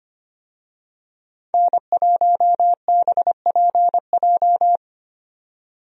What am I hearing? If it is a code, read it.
Morse code, heard as a keyed tone, about 25 wpm, 700 Hz. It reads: N1BPJ
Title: USA Call Signs - 5 Characters - 25wpm